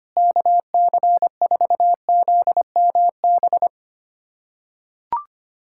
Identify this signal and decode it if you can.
Morse code, heard as a keyed tone, about 25 wpm, 700 Hz. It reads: KC4ZMB
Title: USA Callsign DE Callsign - 6 Characters - 25wpm